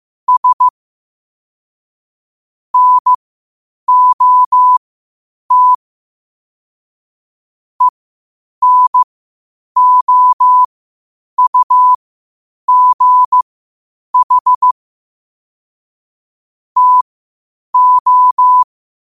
Morse code, heard as a keyed tone, about 15 wpm, spaced as for 8 wpm, 1000 Hz. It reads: S NOT ENOUGH TO
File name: qotdfiles / 8wpm